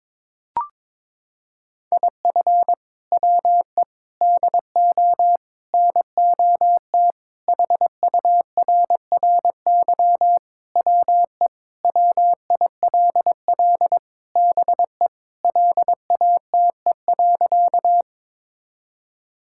Morse code, heard as a keyed tone, about 22 wpm, 700 Hz. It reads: IF WE DO NOT HURRY WE WILL BE LATE.
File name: practice / 22wpm / Sentences with 4-Word Phrases - 22wpm